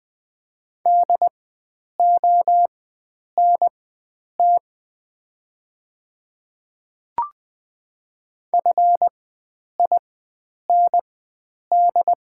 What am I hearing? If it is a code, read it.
Morse code, heard as a keyed tone, about 20 wpm, spaced as for 9 wpm, 690 Hz. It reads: DONT FIND
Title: 4x Character Spacing - Top 300 Words - 20wpm